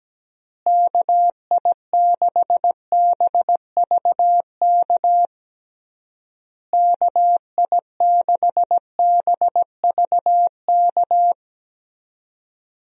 Morse code, heard as a keyed tone, about 17 wpm, 700 Hz. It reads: KI6BVK KI6BVK
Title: USA Call Signs - 6 Characters - Repeated 3x - 17wpm